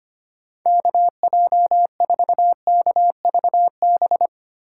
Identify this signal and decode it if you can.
Morse code, heard as a keyed tone, about 25 wpm, 700 Hz. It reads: KJ4KVB
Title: USA Call Signs - 6 Characters - 4x Character Spacing - 25wpm